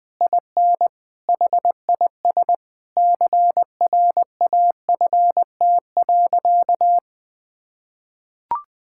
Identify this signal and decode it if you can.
Morse code, heard as a keyed tone, about 20 wpm, 700 Hz. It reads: IN HIS CRAFT.
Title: Sentences from Top 1000 Words - Review Part 6 - 20wpm